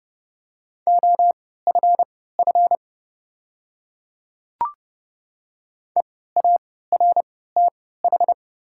Morse code, heard as a keyed tone, about 30 wpm, spaced as for 17 wpm, 700 Hz. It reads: OFF EARTH
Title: 3x Character Spacing - Top 500 Words - 30wpm